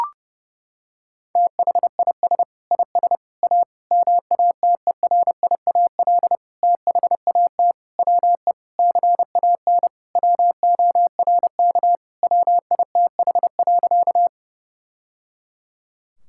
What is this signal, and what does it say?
Morse code, heard as a keyed tone, about 32 wpm, 700 Hz. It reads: THIS IS A MATERIAL THAT WE CAN WORK WITH.
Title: Sentences from Top 1000 Words - Review Part 5 - 30wpm